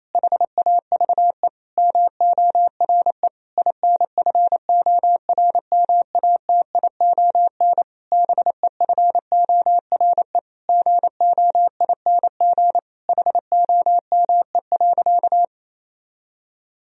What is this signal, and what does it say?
Morse code, heard as a keyed tone, about 28 wpm, 700 Hz. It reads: HAVE MORE INFORMATION BEFORE GOING HOME.
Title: Sentences from Top 1000 Words - Review Part 3 - 28wpm